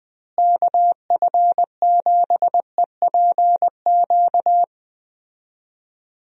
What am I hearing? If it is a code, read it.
Morse code, heard as a keyed tone, about 20 wpm, 700 Hz. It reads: KF7EPQ
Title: USA Call Signs - 6 Characters - Repeated 2x - 20wpm